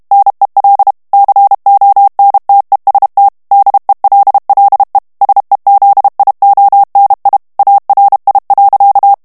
Morse code, heard as a keyed tone, about 32 wpm, 800 Hz. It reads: NEL CONTEST DELLE SEZIONI ARI.
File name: texte clair 2 vitesse 30 mots minute